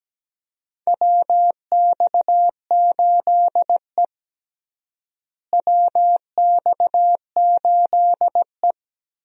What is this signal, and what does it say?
Morse code, heard as a keyed tone, about 17 wpm, 700 Hz. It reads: WX8E WX8E
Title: USA Call Signs - 4 Characters - Repeated 2x - 17wpm